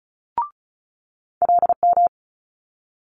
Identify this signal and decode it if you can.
Morse code, heard as a keyed tone, about 36 wpm, 700 Hz. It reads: LM
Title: Multi-letter Phonemes Quick Fire 35wpm